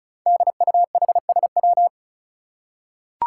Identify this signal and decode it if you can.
Morse code, heard as a keyed tone, about 36 wpm, 700 Hz. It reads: DUHSW
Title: Five Challenging Letter-Number Combinations 35wpm